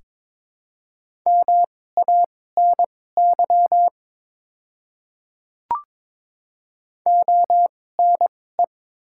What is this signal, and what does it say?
Morse code, heard as a keyed tone, about 22 wpm, spaced as for 16 wpm, 700 Hz. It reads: MANY ONE
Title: 2x Character Spacing - Top 100 Words - 22wpm